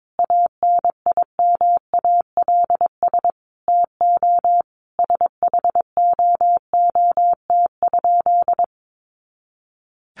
Morse code, heard as a keyed tone, about 22 wpm, 700 Hz. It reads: ANIMALS TO SHOOT?
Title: A Sound Of Thunder 22wpm